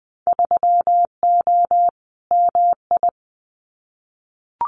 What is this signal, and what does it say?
Morse code, heard as a keyed tone, about 20 wpm, 700 Hz. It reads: 3O MI
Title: Winter Field Day Exchanges 20wpm